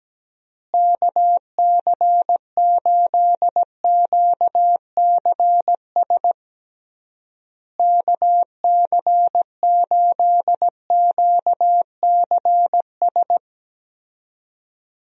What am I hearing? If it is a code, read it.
Morse code, heard as a keyed tone, about 17 wpm, 700 Hz. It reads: KC8QCS KC8QCS
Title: USA Call Signs - 6 Characters - Repeated 3x - 17wpm